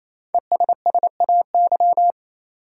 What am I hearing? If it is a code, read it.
Morse code, heard as a keyed tone, about 28 wpm, 700 Hz. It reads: ESSAY
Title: Five Letter Words - 28wpm